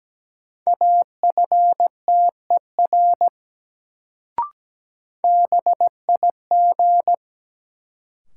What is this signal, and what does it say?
Morse code, heard as a keyed tone, about 17 wpm, 700 Hz. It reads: AFTER BIG
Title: Sentences from Top 200 Words - New Word Drill - 17wpm